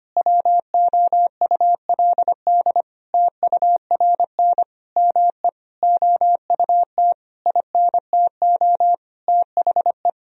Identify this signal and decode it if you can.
Morse code, heard as a keyed tone, about 25 wpm, 700 Hz. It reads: WOULD TURN ME OUT INTO THE
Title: Sentences from Top 1000 Words - Review Part 4 - 25wpm